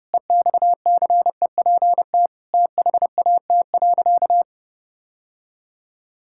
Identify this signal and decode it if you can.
Morse code, heard as a keyed tone, about 30 wpm, 700 Hz. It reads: EXCEPT THAT.
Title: Sentences from Top 1600 Words 30wpm